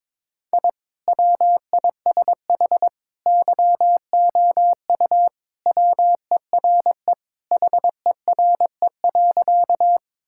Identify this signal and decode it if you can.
Morse code, heard as a keyed tone, about 22 wpm, 700 Hz. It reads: I WISH YOU WERE HERE.
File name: practice / 22wpm / Sentences with 2-Word Phrases - 22wpm